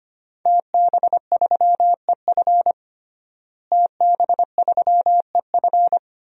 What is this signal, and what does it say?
Morse code, heard as a keyed tone, about 25 wpm, 700 Hz. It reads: TB3EF TB3EF
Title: DXCC-100 Call Signs - 5 Characters - Repeated 3x - 25wpm